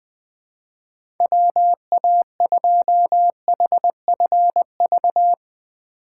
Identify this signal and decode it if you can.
Morse code, heard as a keyed tone, about 21 wpm, 700 Hz. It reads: WA2HFV
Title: USA Call Signs - 6 Characters - 3x Character Spacing - 20wpm